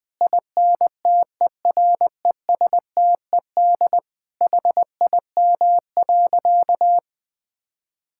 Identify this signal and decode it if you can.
Morse code, heard as a keyed tone, about 21 wpm, 700 Hz. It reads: INTERESTED HIM.
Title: Sentences from Top 1000 Words - Review Part 6 - 20wpm